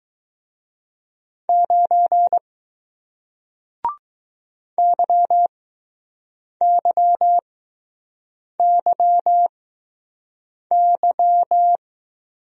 Morse code, heard as a keyed tone, about 20 wpm, spaced as for 6 wpm, 700 Hz. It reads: 9 YYYY